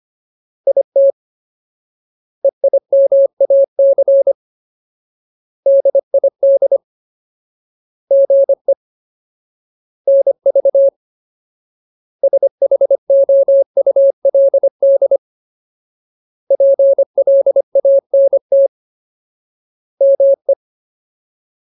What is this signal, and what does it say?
Morse code, heard as a keyed tone, about 25 wpm, 550 Hz. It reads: IT EIMAC DID GE NV SHOULD PLANT ME
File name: Ultimate Word List / 25WPM Ultimate